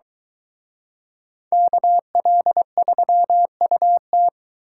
Morse code, heard as a keyed tone, about 23 wpm, 700 Hz. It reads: KL3UT